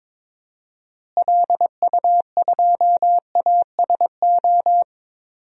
Morse code, heard as a keyed tone, about 22 wpm, 700 Hz. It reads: LU2ASO
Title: DXCC-100 Call Signs - 6 Characters - Repeated 3x - 22wpm